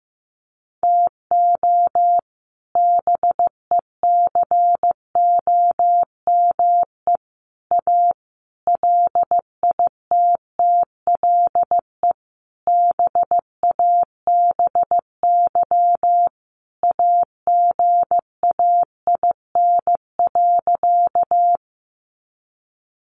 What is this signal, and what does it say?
Morse code, heard as a keyed tone, about 15 wpm, 700 Hz. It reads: TO BECOME A LITTLE BABY AGAIN.
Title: Sentences from Top 1300 Words 15wpm